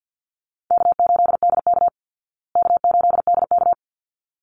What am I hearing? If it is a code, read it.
Morse code, heard as a keyed tone, about 54 wpm, 700 Hz. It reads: K8DK K8DK
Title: Easy Call Signs Repeated 2x - 50wpm